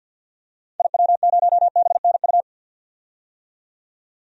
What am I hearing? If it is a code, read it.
Morse code, heard as a keyed tone, about 48 wpm, 700 Hz. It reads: IW0BNU
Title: DXCC-100 Call Signs - 6 Characters - 50wpm